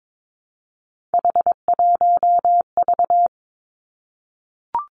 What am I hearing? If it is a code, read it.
Morse code, heard as a keyed tone, about 22 wpm, 700 Hz. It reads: H1V
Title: Three Challenging Letter-Number Combinations 22wpm